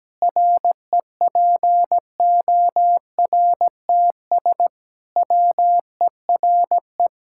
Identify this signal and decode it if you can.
Morse code, heard as a keyed tone, about 17 wpm, 700 Hz. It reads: REPORTS WERE
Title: Pangram Challenge v2 - 17wpm